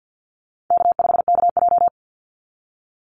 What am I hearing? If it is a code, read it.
Morse code, heard as a keyed tone, about 50 wpm, 700 Hz. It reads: K5KJ
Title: Easy Call Signs Repeated 2x - 50wpm